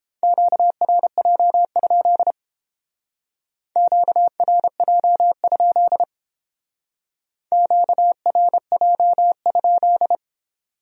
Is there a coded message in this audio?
Yes: Morse code from 0.2 to 10.2 s, heard as a keyed tone, about 31 wpm, 700 Hz. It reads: QRJ? QRJ? QRJ?